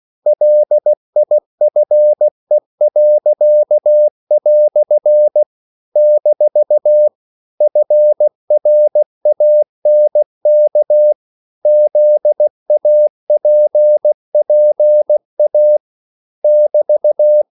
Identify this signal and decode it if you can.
Morse code, heard as a keyed tone, about 16 wpm, 600 Hz. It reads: LIFE." - FRANK ZAPPA <BT>
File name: CW / Practice / EnglishQuotations0216